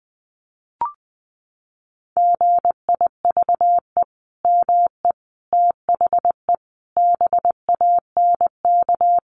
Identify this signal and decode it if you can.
Morse code, heard as a keyed tone, about 20 wpm, 700 Hz. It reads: GIVE ME THE BANK
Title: Sentences from Top 1200 Words 20wpm